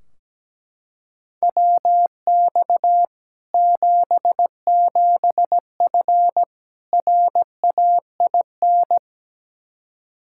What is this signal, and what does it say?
Morse code, heard as a keyed tone, about 17 wpm, 700 Hz. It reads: WX 77F RAIN
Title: Practice QSO - Exchange 2 - Info - 17wpm